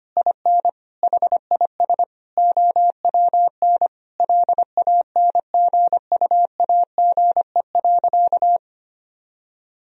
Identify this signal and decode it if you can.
Morse code, heard as a keyed tone, about 25 wpm, 700 Hz. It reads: IN HIS OWN LANGUAGE.
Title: Sentences from Top 1000 Words - Review Part 3 - 25wpm